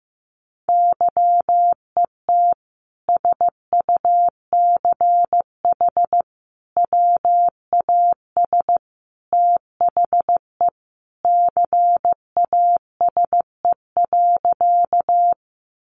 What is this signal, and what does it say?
Morse code, heard as a keyed tone, about 15 wpm, 700 Hz. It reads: YET SUCH WAS THE CASE.
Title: Sentences from Top 700 Words 15wpm